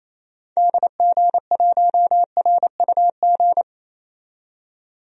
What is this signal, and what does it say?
Morse code, heard as a keyed tone, about 28 wpm, 700 Hz. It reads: DG1RUG
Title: DXCC-100 Call Signs - 6 Characters - 28wpm